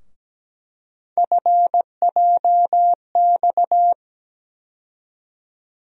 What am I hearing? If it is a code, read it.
Morse code, heard as a keyed tone, about 17 wpm, 700 Hz. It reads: FJX
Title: USA Call Sign Postfixes - 3 Letters 17wpm